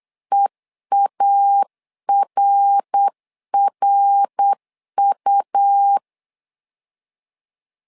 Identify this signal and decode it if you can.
Morse code, heard as a keyed tone, about 8 wpm, 800 Hz. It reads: EARRU
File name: LEZIONI / 3 MWRO / 3 MWR 8wpm 2444